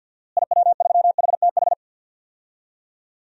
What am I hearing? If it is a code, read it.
Morse code, heard as a keyed tone, about 52 wpm, 700 Hz. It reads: IW3HTH
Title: DXCC-100 Call Signs - 6 Characters - Repeated 3x - 50wpm